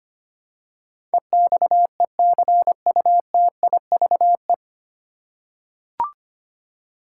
Morse code, heard as a keyed tone, about 25 wpm, 700 Hz. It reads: EXECUTIVE
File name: practice / 25wpm / 9 Letter Words - First 3 Letters Given - 25wpm